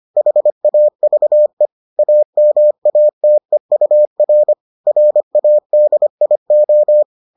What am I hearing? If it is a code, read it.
Morse code, heard as a keyed tone, about 25 wpm, 600 Hz. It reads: HAVE AMATEUR RADIO